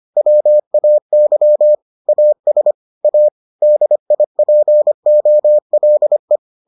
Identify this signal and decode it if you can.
Morse code, heard as a keyed tone, about 25 wpm, 600 Hz. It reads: WAY AS A DIPOLE